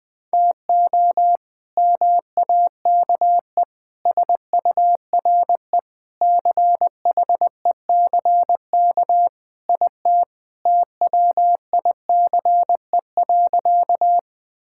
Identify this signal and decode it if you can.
Morse code, heard as a keyed tone, about 20 wpm, 700 Hz. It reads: TO MAKE SURE CHECK IT TWICE.
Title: Sentences with 3-Word Phrases - 20wpm